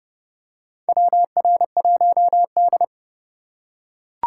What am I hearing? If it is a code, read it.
Morse code, heard as a keyed tone, about 30 wpm, 700 Hz. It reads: WR1D
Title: USA Call Signs - 2x Character Spacing - 30wpm